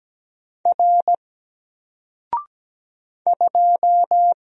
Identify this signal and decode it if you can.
Morse code, heard as a keyed tone, about 17 wpm, 700 Hz. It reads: R 2
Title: Single Letter-Number 17wpm